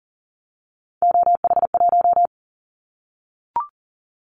Morse code, heard as a keyed tone, about 38 wpm, 700 Hz. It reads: OH1
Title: Callsign Prefixes for Top 50 DXCC - 40wpm